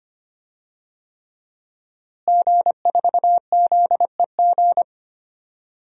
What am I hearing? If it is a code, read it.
Morse code, heard as a keyed tone, about 25 wpm, 700 Hz. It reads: G4ZEG
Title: DXCC-100 Call Signs - 5 Characters - Repeated 3x - 25wpm